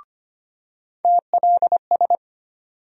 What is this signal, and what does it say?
Morse code, heard as a keyed tone, about 25 wpm, 700 Hz. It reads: TLS
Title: USA Call Sign Postfixes - 3 Letters 25wpm